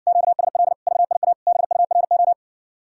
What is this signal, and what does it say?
Morse code, heard as a keyed tone, about 66 wpm, 700 Hz. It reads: QSL VIA BURO